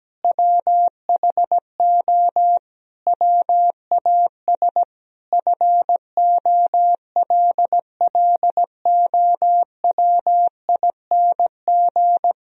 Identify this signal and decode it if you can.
Morse code, heard as a keyed tone, about 17 wpm, 700 Hz. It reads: WHO WAS FOLLOWING
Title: Sentences from Top 1000 Words - Review Part 4 - 17wpm